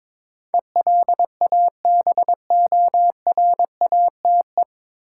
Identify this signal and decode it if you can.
Morse code, heard as a keyed tone, about 22 wpm, 700 Hz. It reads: ELABORATE
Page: Morse Code Ninja practice 22wpm